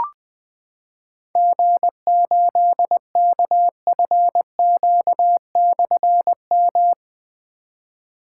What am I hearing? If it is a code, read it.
Morse code, heard as a keyed tone, about 20 wpm, 700 Hz. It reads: G8KFQ/M
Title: Difficult Call Signs 20wpm